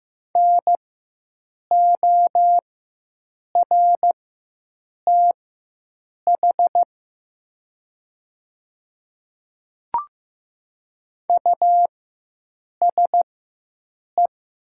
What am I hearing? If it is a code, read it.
Morse code, heard as a keyed tone, about 15 wpm, spaced as for 7 wpm, 690 Hz. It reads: NORTH USE
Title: 4x Character Spacing - Top 500 Words - 15wpm